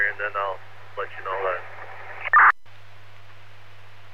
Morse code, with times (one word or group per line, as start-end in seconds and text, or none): none